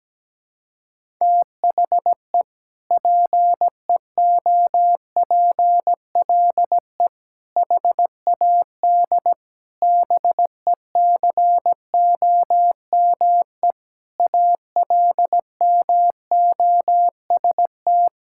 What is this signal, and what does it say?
Morse code, heard as a keyed tone, about 17 wpm, 700 Hz. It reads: THE PEOPLE HAD BECOME ALMOST
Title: Sentences from Top 1000 Words - Review Part 6 - 17wpm